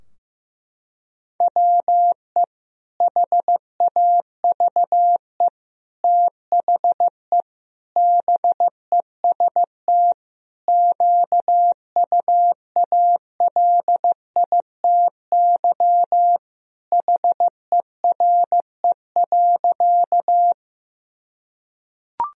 Morse code, heard as a keyed tone, about 15 wpm, 700 Hz. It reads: WE HAVE THE BEST QUALITY HERE.
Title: Sentences from Top 1000 Words - Review Part 4 - 15wpm